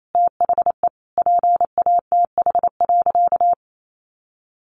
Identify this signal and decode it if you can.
Morse code, heard as a keyed tone, about 28 wpm, 700 Hz. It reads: THE PATH.
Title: A Sound Of Thunder 28wpm